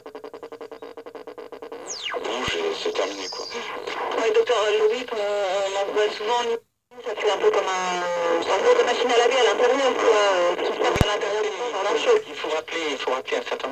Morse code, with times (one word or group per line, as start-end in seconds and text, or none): none